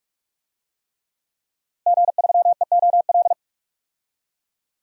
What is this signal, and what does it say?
Morse code, heard as a keyed tone, about 43 wpm, 700 Hz. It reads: G3EOL
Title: DXCC-100 Call Signs - 5 Characters - Repeated 3x - 45wpm